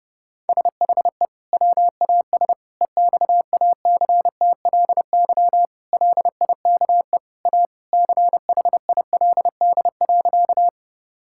Morse code, heard as a keyed tone, about 31 wpm, 700 Hz. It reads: SHE WAS EXACTLY LIKE A CHILD.
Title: Sentences from Top 1000 Words - Review Part 6 - 30wpm